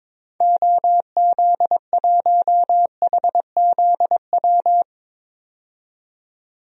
Morse code, heard as a keyed tone, about 21 wpm, 700 Hz. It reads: OZ1HZW
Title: DXCC-100 Call Signs - 6 Characters - 22wpm